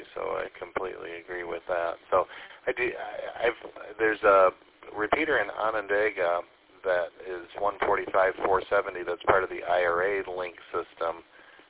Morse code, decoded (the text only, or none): none